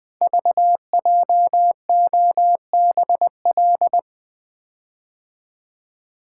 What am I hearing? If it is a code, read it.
Morse code, heard as a keyed tone, about 20 wpm, 700 Hz. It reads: VJOBL
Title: Code Groups - 5 Letters - 20wpm